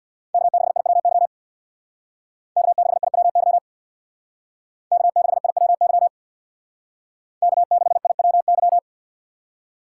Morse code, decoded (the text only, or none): K6IWY K6IWY K6IWY K6IWY